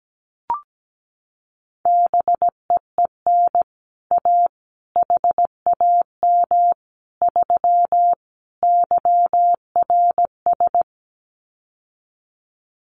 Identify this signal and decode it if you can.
Morse code, heard as a keyed tone, about 17 wpm, 700 Hz. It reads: BEEN A HAM 3 YRS
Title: Common QSO Phrases Encore 17wpm